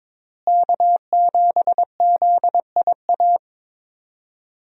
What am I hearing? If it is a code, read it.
Morse code, heard as a keyed tone, about 22 wpm, 700 Hz. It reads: K7ZIA
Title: USA Call Signs - 5 Characters - Repeated 2x - 22wpm